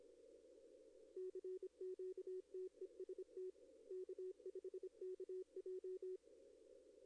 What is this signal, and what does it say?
Morse code, heard as a keyed tone, about 25 wpm, 370 Hz. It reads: CQ TEST K5KJ
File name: contests / arrl-160-k5tr-op-w5tn -1300